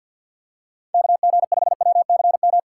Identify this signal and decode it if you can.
Morse code, heard as a keyed tone, about 53 wpm, 700 Hz. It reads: KG5WCM
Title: USA Call Signs - 2x Character Spacing - 50wpm